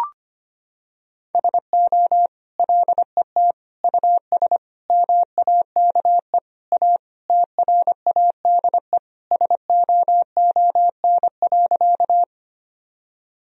Morse code, heard as a keyed tone, about 25 wpm, 700 Hz. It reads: SO LET US MAKE A TRADE SOON.
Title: Sentences from Top 1000 Words - Review Part 4 - 25wpm